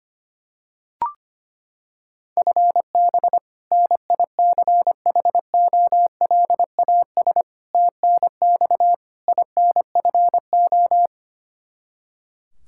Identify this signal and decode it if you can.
Morse code, heard as a keyed tone, about 25 wpm, 700 Hz. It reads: FB NICHOLAS TNX INFO
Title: K1USN SST QSO Practice 25wpm